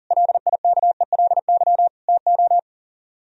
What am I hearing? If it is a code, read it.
Morse code, heard as a keyed tone, about 49 wpm, 700 Hz. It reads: LIKELY TO
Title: Speed-Racing - Sets of 4 Words 40wpm